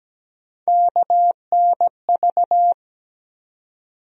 Morse code, heard as a keyed tone, about 17 wpm, 700 Hz. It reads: KNV
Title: USA Call Sign Postfixes - 3 Letters 17wpm